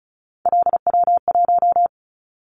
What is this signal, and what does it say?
Morse code, heard as a keyed tone, about 36 wpm, 700 Hz. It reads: LW1
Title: Callsign Prefixes for Top 100 DXCC - 35wpm